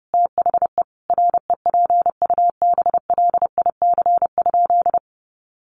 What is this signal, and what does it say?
Morse code, heard as a keyed tone, about 30 wpm, 700 Hz. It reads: THE REPUBLIC?
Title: Sentences from Top 1500 Words 30wpm